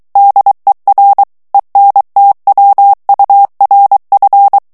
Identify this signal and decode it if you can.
Morse code, heard as a keyed tone, about 23 wpm, 800 Hz. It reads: DER ENTWURF
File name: ALLEMAND texte clair 1 vitesse 22 mots minute